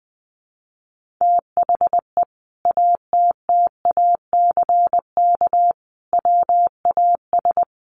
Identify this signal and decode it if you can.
Morse code, heard as a keyed tone, about 20 wpm, 700 Hz. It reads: THE ATTACK WAS
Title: Sentences from Top 1500 Words 20wpm